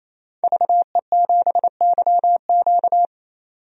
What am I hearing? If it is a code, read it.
Morse code, heard as a keyed tone, about 28 wpm, 700 Hz. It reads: VE7YQ
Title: DXCC-100 Call Signs - 5 Characters - 28wpm